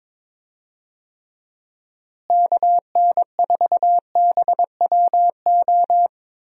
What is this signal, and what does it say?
Morse code, heard as a keyed tone, about 22 wpm, 700 Hz. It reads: KN4BWO
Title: USA Call Signs - 6 Characters - Repeated 2x - 22wpm